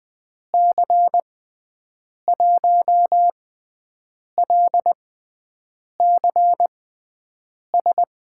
Morse code, heard as a keyed tone, about 20 wpm, spaced as for 7 wpm, 700 Hz. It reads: C1LCS
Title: USA Call Signs - 6 Characters - 6x Character Spacing - 20wpm